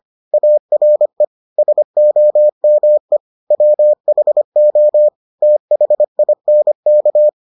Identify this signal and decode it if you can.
Morse code, heard as a keyed tone, about 25 wpm, 600 Hz. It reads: ARE SOME WHO THINK